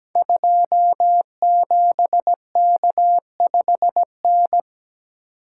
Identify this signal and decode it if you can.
Morse code, heard as a keyed tone, about 17 wpm, 700 Hz. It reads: 27K5N
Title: Code Groups - 5 Letter:Numbers - 17wpm